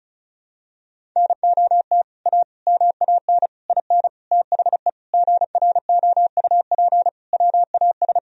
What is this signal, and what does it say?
Morse code, heard as a keyed tone, about 36 wpm, 700 Hz. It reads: NOT A MAN IN THE GROUP WAS
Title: Sentences from Top 1000 Words - Review Part 1 - 35wpm